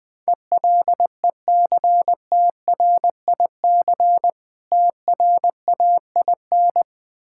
Morse code, heard as a keyed tone, about 20 wpm, 700 Hz. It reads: ELECTRIC TRAIN
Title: Difficult Words with Hints - 20wpm